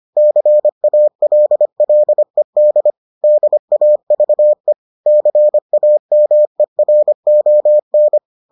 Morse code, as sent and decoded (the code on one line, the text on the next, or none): -.-. .- .-.. .-.. -.. / -.. .- ...- . / -.-. .- -- . .-. --- -.
CALLED DAVE CAMERON